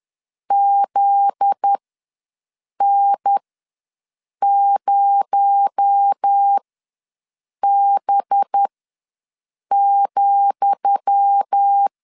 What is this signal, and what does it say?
Morse code, heard as a keyed tone, about 10 wpm, spaced as for 6 wpm, 800 Hz. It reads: ZN0B,
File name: LEZIONI / 10 SIGNS / 10D COMA ws 10wpm 4314